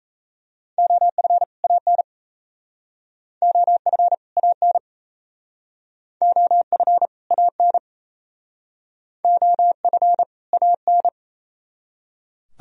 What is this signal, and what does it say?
Morse code, heard as a keyed tone, about 37 wpm, 700 Hz. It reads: OF AN OF AN OF AN OF AN